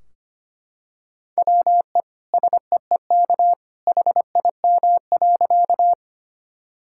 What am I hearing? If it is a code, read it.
Morse code, heard as a keyed tone, about 25 wpm, 700 Hz. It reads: WE SEEK HIM.
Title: Sentences from Top 1600 Words 25wpm